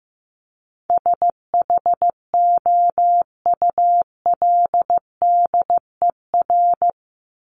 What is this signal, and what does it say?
Morse code, heard as a keyed tone, about 15 wpm, 700 Hz. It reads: SHOULDER
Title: Top 1501-1750 Words 15wpm